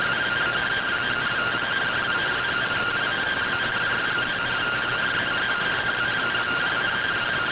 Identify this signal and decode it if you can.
Two-tone teleprinter signal, about 170 Hz shift.